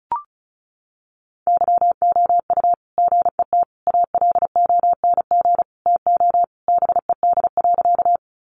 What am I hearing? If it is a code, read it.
Morse code, heard as a keyed tone, about 34 wpm, 700 Hz. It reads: YOU GET ALONG TO BED.
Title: Sentences from Top 700 Words 35wpm